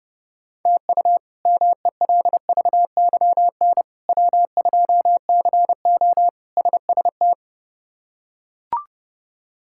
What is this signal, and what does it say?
Morse code, heard as a keyed tone, about 30 wpm, 700 Hz. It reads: TU MELVYN W2CO SST
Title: K1USN SST QSO Practice - 30wpm